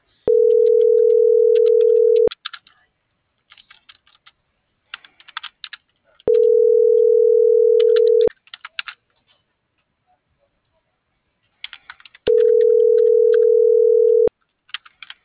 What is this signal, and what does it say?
Ringback tone (North America), 440 + 480 Hz.
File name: CENTRAL MAYAB / out lizbeth.mayab - .30392